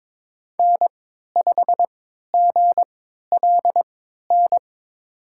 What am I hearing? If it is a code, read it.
Morse code, heard as a keyed tone, about 22 wpm, spaced as for 13 wpm, 700 Hz. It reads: N5GLN
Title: USA Call Signs - 5 Characters - 3x Character Spacing - 22wpm